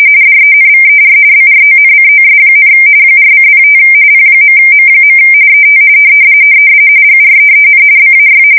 RTTY, about 45.45 baud. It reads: DENVER BRONCOS, 4:05 P.M. ET GREEN BAY
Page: 170 Shift 2125/2295